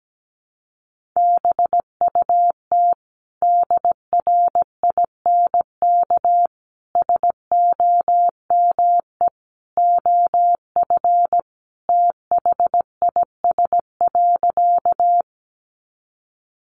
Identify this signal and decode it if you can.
Morse code, heard as a keyed tone, about 17 wpm, 700 Hz. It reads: BUT DRINK SOME OF THIS.
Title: Sentences from Top 1500 Words 17wpm